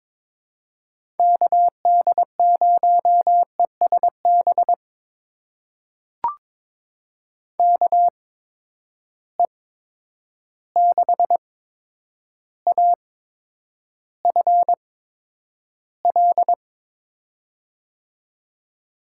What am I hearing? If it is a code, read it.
Morse code, heard as a keyed tone, about 22 wpm, spaced as for 9 wpm, 700 Hz. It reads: KD0ESB KE6AFL
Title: USA Call Signs - 6 Characters - 8x Character Spacing - 22wpm